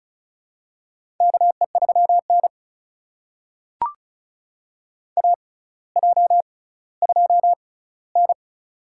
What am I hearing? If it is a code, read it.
Morse code, heard as a keyed tone, about 34 wpm, spaced as for 18 wpm, 700 Hz. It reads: KE3N AJ2N
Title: USA Call Signs - 4 Characters - 6x Character Spacing - 35wpm